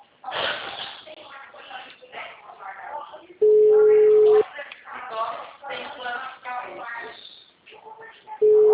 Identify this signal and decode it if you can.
Ringback tone (Europe, 425 Hz), with speech.